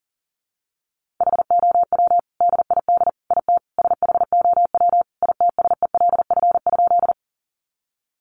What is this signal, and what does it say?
Morse code, heard as a keyed tone, about 37 wpm, 700 Hz. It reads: HOW DID IT SHOW ITSELF?
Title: Sentences from Top 800 Words 40wpm